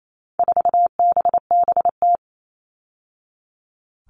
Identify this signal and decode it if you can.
Morse code, heard as a keyed tone, about 28 wpm, 700 Hz. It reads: 4BBT
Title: All Cut Numbers - 4-Digit 28wpm